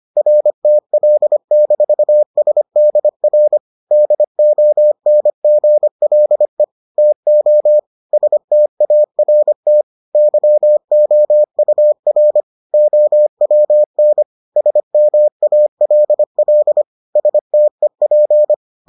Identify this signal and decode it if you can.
Morse code, heard as a keyed tone, about 25 wpm, 600 Hz. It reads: RTL-SDR DONGLE TO START YOUR OWN SMALL STEP